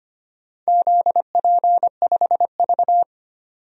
Morse code, heard as a keyed tone, about 25 wpm, 700 Hz. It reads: ZP5V
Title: DXCC-100 Call Signs - 4 Characters - 25wpm